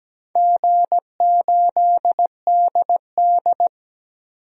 Morse code, heard as a keyed tone, about 17 wpm, 700 Hz. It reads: G8DD
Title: DXCC-100 Call Signs - 4 Characters - 17wpm